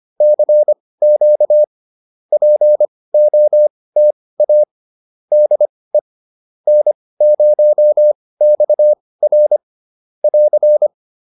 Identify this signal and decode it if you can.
Morse code, heard as a keyed tone, about 25 wpm, spaced as for 18 wpm, 600 Hz. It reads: CQ POTA DE N0XR <AR>